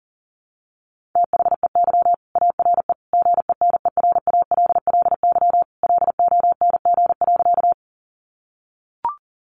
Morse code, heard as a keyed tone, about 40 wpm, 700 Hz. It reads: THEY ARE GENERALLY LONG.
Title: Sentences from Top 600 Words 40wpm